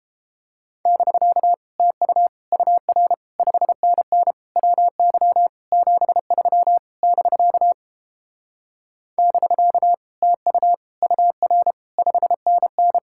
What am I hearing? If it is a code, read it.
Morse code, heard as a keyed tone, about 29 wpm, 700 Hz. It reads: <BK> TU UR 5NN WY 73 <BK> <BK> TU UR 5NN